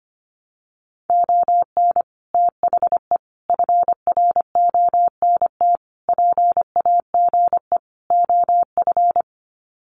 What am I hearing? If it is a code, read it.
Morse code, heard as a keyed tone, about 25 wpm, 700 Hz. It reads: ON THE FRONT PAGE OF
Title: Sets of 5 Words Encore 25wpm